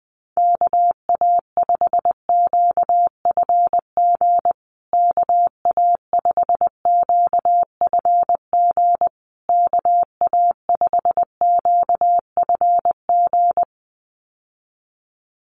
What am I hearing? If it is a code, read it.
Morse code, heard as a keyed tone, about 20 wpm, 700 Hz. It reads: KA5QFG KA5QFG KA5QFG
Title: USA Call Signs Repeated 3x 20wpm